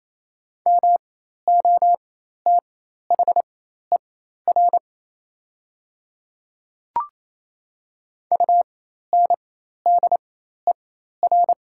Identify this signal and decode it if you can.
Morse code, heard as a keyed tone, about 28 wpm, spaced as for 13 wpm, 700 Hz. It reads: MOTHER UNDER
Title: 4x Character Spacing - Top 300 Words - 28wpm